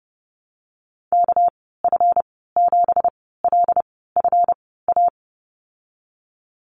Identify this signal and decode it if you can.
Morse code, heard as a keyed tone, about 30 wpm, spaced as for 17 wpm, 700 Hz. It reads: KF7LFA
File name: Lesson 37g - Letter x - Call Signs 30wpm